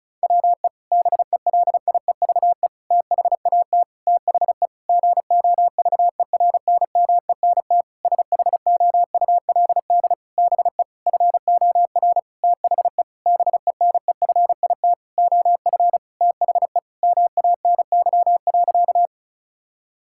Morse code, heard as a keyed tone, about 36 wpm, 700 Hz. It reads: WE BELIEVE THAT THE GOVERNMENT SHOULD BE FOR THE BENEFIT OF THE MANY.
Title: Sentences from Top 1000 Words - Review Part 6 - 35wpm